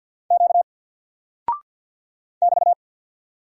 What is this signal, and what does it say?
Morse code, heard as a keyed tone, about 50 wpm, 700 Hz. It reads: Q <BT>
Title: Single Character (Letter:Number:Punctuation:Prosign) 50wpm